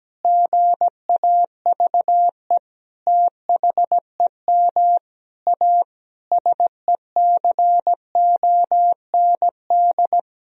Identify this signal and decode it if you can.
Morse code, heard as a keyed tone, about 17 wpm, 700 Hz. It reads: GAVE THEM A SECOND
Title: Sentences from Top 1000 Words - Review Part 1 - 17wpm